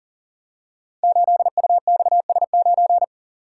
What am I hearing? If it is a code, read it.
Morse code, heard as a keyed tone, about 42 wpm, 700 Hz. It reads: 8UXS9
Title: Code Groups - 5 Letter:Numbers - 40wpm